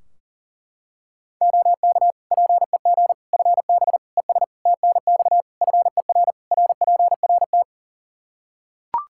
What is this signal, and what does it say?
Morse code, heard as a keyed tone, about 40 wpm, 700 Hz. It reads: OK PEG FB ES TNX FER RPRT
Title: Practice QSO - Exchange 2 - Info - 40wpm